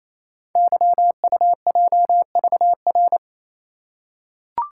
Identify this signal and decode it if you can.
Morse code, heard as a keyed tone, about 28 wpm, 700 Hz. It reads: YUJVR
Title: Code Groups - 5 Letters - 28wpm